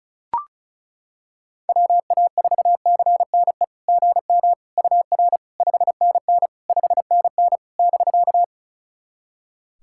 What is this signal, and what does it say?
Morse code, heard as a keyed tone, about 36 wpm, 700 Hz. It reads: WA4CNE GM UR 5NN 5NN <BK>
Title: SOTA QSO Practice 35wpm